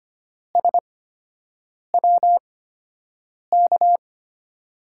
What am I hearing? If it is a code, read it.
Morse code, heard as a keyed tone, about 25 wpm, spaced as for 7 wpm, 700 Hz. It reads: SWK
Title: USA Call Signs - 5 Characters - 8x Character Spacing - 25wpm